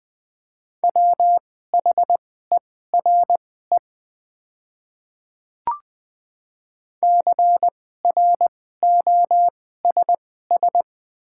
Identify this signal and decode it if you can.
Morse code, heard as a keyed tone, about 20 wpm, spaced as for 14 wpm, 700 Hz. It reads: WHERE CROSS
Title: 2x Character Spacing - Top 300 Words - 20wpm